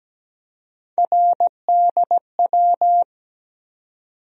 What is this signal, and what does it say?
Morse code, heard as a keyed tone, about 17 wpm, 700 Hz. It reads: RDW
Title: USA Call Sign Postfixes - 3 Letters 17wpm